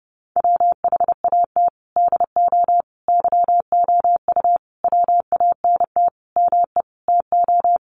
Morse code, heard as a keyed tone, about 30 wpm, 700 Hz. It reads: WHAT DO YOU WANT ME TO
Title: Sentences from Top 200 Words 30wpm